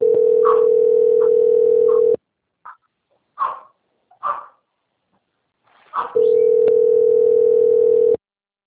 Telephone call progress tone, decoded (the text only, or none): ringback tone